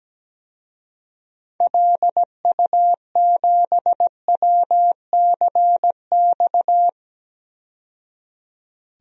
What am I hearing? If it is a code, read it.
Morse code, heard as a keyed tone, about 17 wpm, 700 Hz. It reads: LU7WCX
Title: DXCC-100 Call Signs - 6 Characters - 17wpm